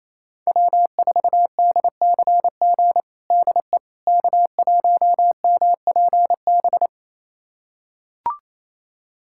Morse code, heard as a keyed tone, about 28 wpm, 700 Hz. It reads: W4DCG DE K1MPB
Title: USA Callsign DE Callsign - 5 Characters - 28wpm